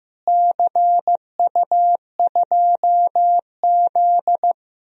Morse code, heard as a keyed tone, about 15 wpm, 700 Hz. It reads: CU2Z
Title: DXCC-100 Call Signs - 4 Characters - Repeated 2x - 15wpm